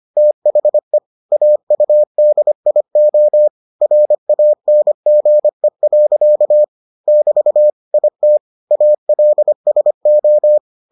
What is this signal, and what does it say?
Morse code, heard as a keyed tone, about 25 wpm, 600 Hz. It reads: THE AUDIO RANGE. <BT> IT ALSO